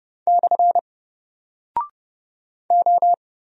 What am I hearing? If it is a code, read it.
Morse code, heard as a keyed tone, about 30 wpm, 700 Hz. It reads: / O
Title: Single Character (Letter:Number:Punctuation:Prosign) 30wpm